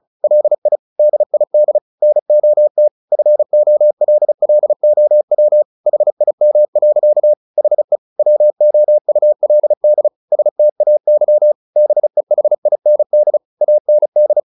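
Morse code, heard as a keyed tone, about 37 wpm, 600 Hz. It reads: LI DID NOT FOLLOW HIM. HE WOULD STAY BEHIND AND
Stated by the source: Morse practice at 35 wpm from CW med Gnister, sending Ken Follett's novel Never in English.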